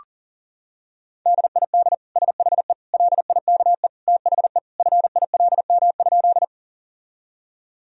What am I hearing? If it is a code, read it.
Morse code, heard as a keyed tone, about 40 wpm, 700 Hz. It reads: DID SHE LIKE THE FILM?
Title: Sentences from Top 1000 Words - Review Part 3 - 40wpm